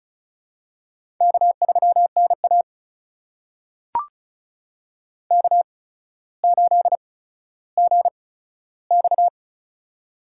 Morse code, heard as a keyed tone, about 36 wpm, spaced as for 15 wpm, 700 Hz. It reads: K3NA K8GX
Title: USA Call Signs - 8x Character Spacing - 35wpm